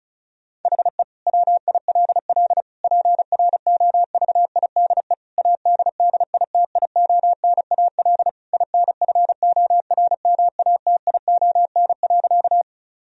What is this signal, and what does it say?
Morse code, heard as a keyed tone, about 34 wpm, 700 Hz. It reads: HE WILL PROVIDE ADDITIONAL INFORMATION.
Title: Sentences from Top 1000 Words - Review Part 4 - 35wpm